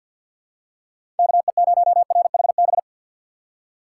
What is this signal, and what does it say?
Morse code, heard as a keyed tone, about 47 wpm, 700 Hz. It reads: KE0RHB
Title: USA Call Signs - 2x Character Spacing - 50wpm